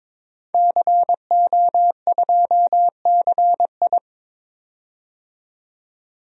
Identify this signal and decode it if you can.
Morse code, heard as a keyed tone, about 22 wpm, 700 Hz. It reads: CO2CI
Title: DXCC-100 Call Signs - 5 Characters - 22wpm